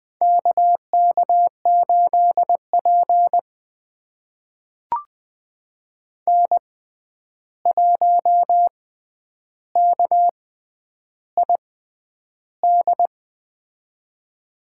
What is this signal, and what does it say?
Morse code, heard as a keyed tone, about 20 wpm, spaced as for 10 wpm, 700 Hz. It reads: KK8P N1KID
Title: USA Call Signs - 6x Character Spacing - 20wpm